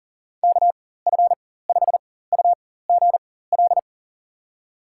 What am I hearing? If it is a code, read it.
Morse code, heard as a keyed tone, about 40 wpm, spaced as for 19 wpm, 700 Hz. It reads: KF5UGL
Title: USA Call Signs - 6 Characters - 4x Character Spacing - 40wpm